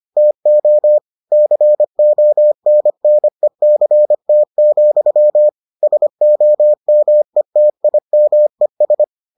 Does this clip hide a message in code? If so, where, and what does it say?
Yes: Morse code from 0.2 to 9.0 s, heard as a keyed tone, about 25 wpm, 600 Hz. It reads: TO CONNECT, SOMETIMES